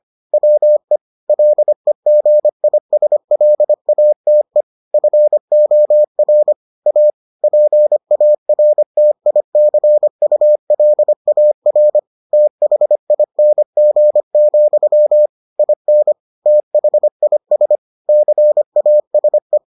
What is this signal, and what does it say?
Morse code, heard as a keyed tone, about 25 wpm, 600 Hz. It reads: WE LEGISLATE FOR A PARTICULAR THING, IN THIS CASE